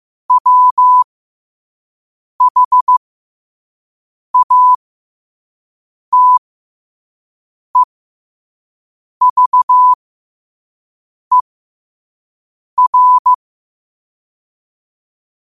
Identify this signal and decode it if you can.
Morse code, heard as a keyed tone, about 15 wpm, spaced as for 5 wpm, 1000 Hz. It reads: WHATEVER
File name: qotdfiles / 5wpm